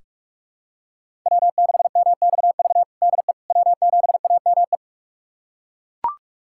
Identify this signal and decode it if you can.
Morse code, heard as a keyed tone, about 42 wpm, 700 Hz. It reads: W6MXV DE W7AME
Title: USA Callsign DE Callsign - 5 Characters - 45wpm